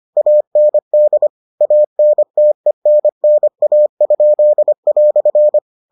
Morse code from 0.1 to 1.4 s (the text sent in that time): AND